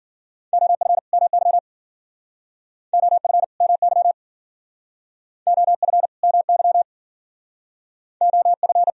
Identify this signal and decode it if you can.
Morse code, heard as a keyed tone, about 54 wpm, 690 Hz. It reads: OF MY OF MY OF MY OF